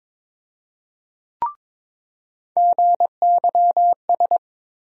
Morse code, heard as a keyed tone, about 22 wpm, 700 Hz. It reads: GYS
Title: USA Call Sign Postfixes - 3 Letters 22wpm